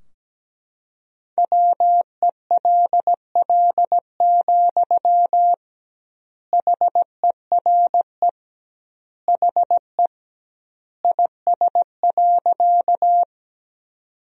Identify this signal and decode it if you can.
Morse code, heard as a keyed tone, about 17 wpm, 700 Hz. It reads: WELL, HERE HE IS.
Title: 2x Word-Spacing - Sentences from Top 200 Words - 17wpm